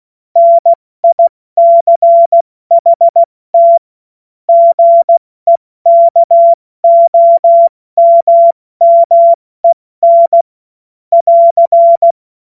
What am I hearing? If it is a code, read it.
Morse code, heard as a keyed tone, about 16 wpm, 680 Hz. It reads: NICHT GEKOMMEN <AR>
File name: Morsekurs / Witze / Witz 05 März 2024 70 F80